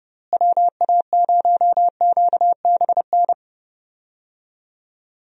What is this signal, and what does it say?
Morse code, heard as a keyed tone, about 30 wpm, 700 Hz. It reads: WA0QBN
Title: USA Call Signs - 6 Characters - Repeated 3x - 30wpm